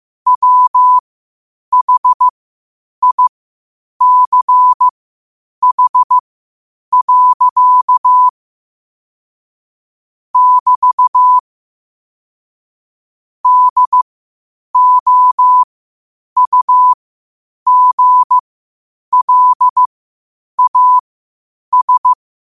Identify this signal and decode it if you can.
Morse code, heard as a keyed tone, about 15 wpm, spaced as for 8 wpm, 1000 Hz. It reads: WHICH. <BT> DOUGLAS
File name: qotdfiles / 8wpm